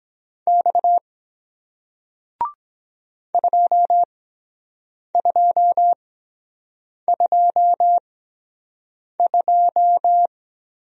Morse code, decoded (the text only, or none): X 2222